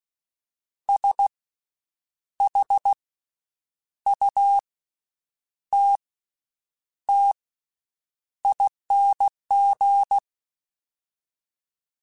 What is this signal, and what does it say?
Morse code, heard as a keyed tone, about 16 wpm, 800 Hz. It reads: S H U T T ING
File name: Special INGs-mp3s / Special INGs-mp3s / Slow ing @15